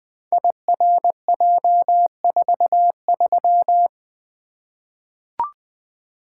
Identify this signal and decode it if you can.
Morse code, heard as a keyed tone, about 20 wpm, 700 Hz. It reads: IRJ43
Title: Code Groups - 5 Letter:Numbers - 20wpm